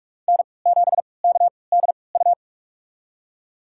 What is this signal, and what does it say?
Morse code, heard as a keyed tone, about 46 wpm, spaced as for 26 wpm, 700 Hz. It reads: N7KDU
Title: USA Call Signs - 5 Characters - 3x Character Spacing - 45wpm